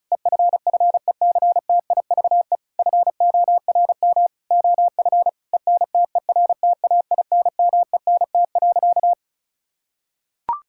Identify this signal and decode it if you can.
Morse code, heard as a keyed tone, about 36 wpm, 700 Hz. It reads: EFFECTIVE FORM OF ENTERTAINMENT.
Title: Sentences from Top 1000 Words - Review Part 6 - 35wpm